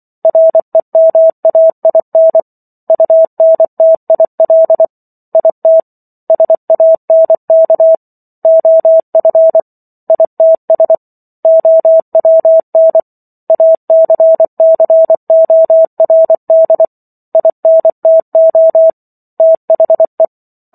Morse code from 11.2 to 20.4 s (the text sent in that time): OWN ACCORD INTO THE